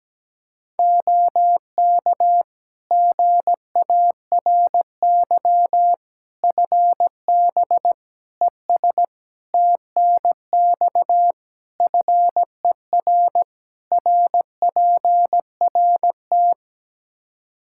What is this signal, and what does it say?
Morse code, heard as a keyed tone, about 17 wpm, 700 Hz. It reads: OK GARY FB ES TNX FER RPRT
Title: Practice QSO - Exchange 2 - Info - 17wpm